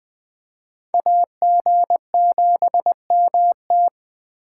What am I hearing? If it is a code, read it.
Morse code, heard as a keyed tone, about 20 wpm, 700 Hz. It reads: AG7MT
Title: Speed Racing - USA 2x2 Call Signs 17wpm